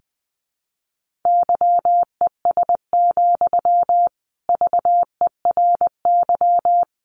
Morse code, heard as a keyed tone, about 20 wpm, 700 Hz. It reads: YES, VERY
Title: Sentences from Top 600 Words 20wpm